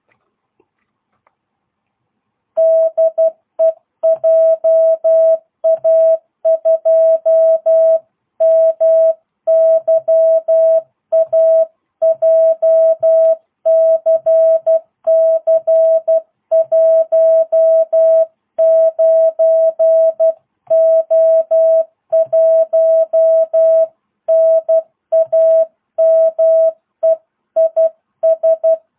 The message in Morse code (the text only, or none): DE JA2MYA JCC 19O1 NAME IS